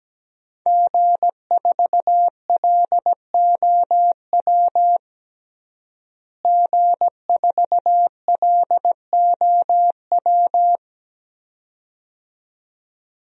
Morse code, heard as a keyed tone, about 17 wpm, 700 Hz. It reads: G4LOW G4LOW
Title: DXCC-100 Call Signs - 5 Characters - Repeated 2x - 17wpm